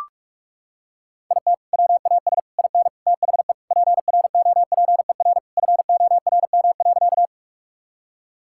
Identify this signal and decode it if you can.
Morse code, heard as a keyed tone, about 42 wpm, 700 Hz. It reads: IT WAS IN THE PROPER FORM.
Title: Sentences from Top 1000 Words - Review Part 5 - 45wpm